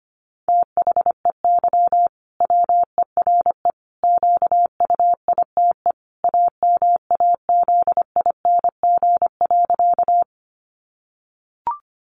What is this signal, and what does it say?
Morse code, heard as a keyed tone, about 25 wpm, 700 Hz. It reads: THEY WERE QUITE AMAZING.
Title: Sentences from Top 1500 Words 25wpm